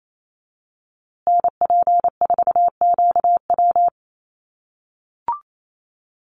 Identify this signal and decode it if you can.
Morse code, heard as a keyed tone, about 28 wpm, 700 Hz. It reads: NP4QW
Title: USA Call Signs Repeated 3x 28wpm